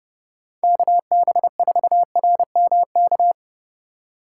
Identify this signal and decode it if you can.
Morse code, heard as a keyed tone, about 29 wpm, 700 Hz. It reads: KB4RMK